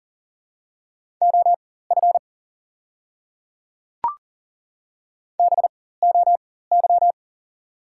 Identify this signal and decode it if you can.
Morse code, heard as a keyed tone, about 39 wpm, spaced as for 18 wpm, 690 Hz. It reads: OF BOY
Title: 4x Character Spacing - Top 200 Words - 40wpm